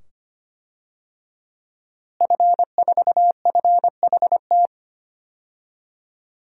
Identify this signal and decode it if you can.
Morse code, heard as a keyed tone, about 25 wpm, 700 Hz. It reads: F4FHT
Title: DXCC-100 Call Signs - 5 Characters - Repeated 2x - 25wpm